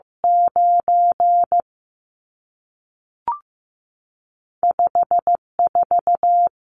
Morse code, heard as a keyed tone, about 15 wpm, 700 Hz. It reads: 9 54
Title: Numbers - 2-Digit 15wpm